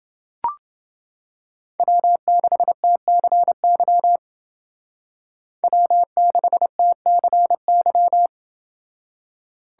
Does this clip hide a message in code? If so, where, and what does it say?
Yes: Morse code from 1.8 to 8.3 s, heard as a keyed tone, about 28 wpm, 700 Hz. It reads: W6TCY W6TCY